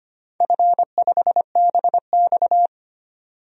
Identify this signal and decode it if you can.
Morse code, heard as a keyed tone, about 25 wpm, 700 Hz. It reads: F5BX
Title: DXCC-100 Call Signs - 4 Characters - 25wpm